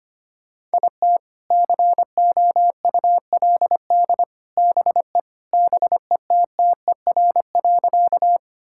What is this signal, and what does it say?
Morse code, heard as a keyed tone, about 25 wpm, 700 Hz. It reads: IT COULD BE BETTER.
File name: Sentences with 2-Word Phrases - 25wpm